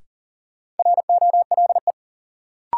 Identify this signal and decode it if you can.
Morse code, heard as a keyed tone, about 40 wpm, 700 Hz. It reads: ROLE
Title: Sentences from Top 1500 Words - New Word Drill Alt - 40wpm